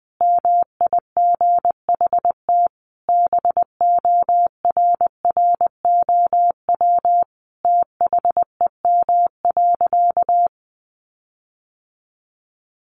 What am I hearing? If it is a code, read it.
Morse code, heard as a keyed tone, about 20 wpm, 700 Hz. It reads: MIGHT BORROW THEM.
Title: Sentences from Top 1500 Words 20wpm